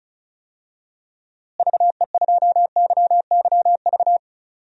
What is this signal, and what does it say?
Morse code, heard as a keyed tone, about 36 wpm, 700 Hz. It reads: VE2YYV
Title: DXCC-100 Call Signs - 6 Characters - Repeated 2x - 35wpm